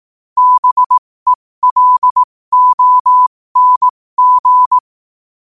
Morse code, heard as a keyed tone, about 18 wpm, spaced as for 15 wpm, 1000 Hz. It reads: BELONG
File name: qotdfiles / 15wpm